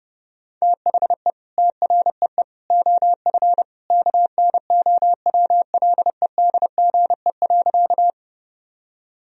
Morse code, heard as a keyed tone, about 30 wpm, 700 Hz. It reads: THE TREE OF KNOWLEDGE.
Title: Sentences from Top 1000 Words - Review Part 3 - 30wpm